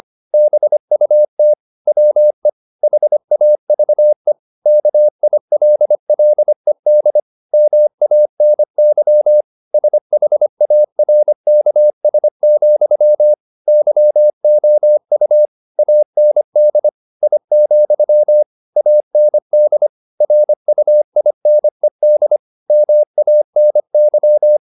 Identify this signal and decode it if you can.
Morse code, heard as a keyed tone, about 25 wpm, 600 Hz. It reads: BUT WE HAVE KILLED MANY SHARKS, YOU AND I, AND RUINED MANY